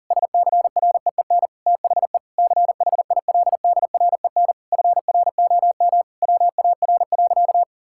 Morse code, heard as a keyed tone, about 40 wpm, 700 Hz. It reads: SCREEN THE CHILDREN FROM WAR.
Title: Sentences from Top 1000 Words - Review Part 6 - 40wpm